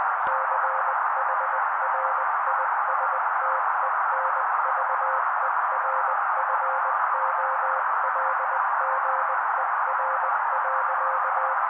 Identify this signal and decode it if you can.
Morse code, heard as a keyed tone, about 20 wpm, 530 Hz. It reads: CHRISTENVERFOLGER.
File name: daily / 18wpm noise